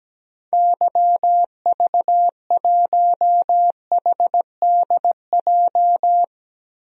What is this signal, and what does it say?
Morse code, heard as a keyed tone, about 17 wpm, 700 Hz. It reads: YV1HDJ
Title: DXCC-100 Call Signs - 6 Characters - 17wpm